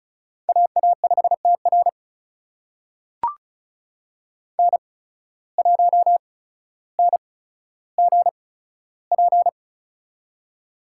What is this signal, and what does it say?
Morse code, heard as a keyed tone, about 34 wpm, spaced as for 15 wpm, 700 Hz. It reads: AA5TR N1NGP
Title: USA Call Signs - 8x Character Spacing - 35wpm